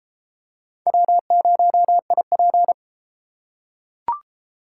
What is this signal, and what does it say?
Morse code, heard as a keyed tone, about 33 wpm, 700 Hz. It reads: W0IP